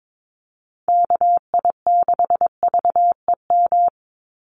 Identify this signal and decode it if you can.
Morse code, heard as a keyed tone, about 22 wpm, 700 Hz. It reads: KI6VEM
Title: USA Call Signs 22wpm